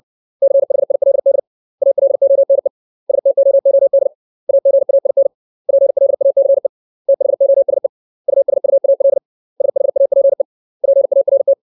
Morse code, heard as a keyed tone, about 60 wpm, 550 Hz. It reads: CHILD ALONE STOOD AGAIN PLACE THOSE USUAL SHAPE PART
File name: 632 words / 632 Words 60WPM